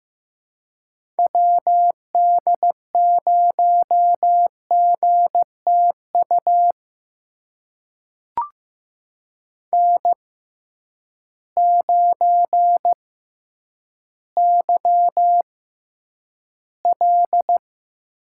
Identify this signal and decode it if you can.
Morse code, heard as a keyed tone, about 15 wpm, spaced as for 9 wpm, 700 Hz. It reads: WD0GTU N9YL
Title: USA Call Signs - 6x Character Spacing - 15wpm